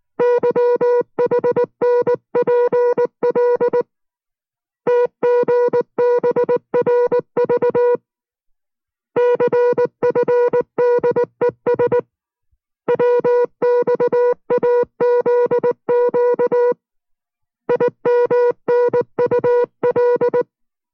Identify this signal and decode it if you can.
Morse code, heard as a keyed tone, about 19 wpm, 490 Hz. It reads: YHNPL TGBRV CFDES WXAZQ IMNUL